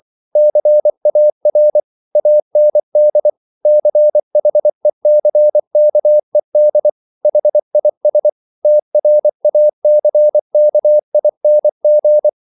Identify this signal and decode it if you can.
Morse code, heard as a keyed tone, about 24 wpm, 600 Hz. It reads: CAR AND CHECKED HIS TRACKING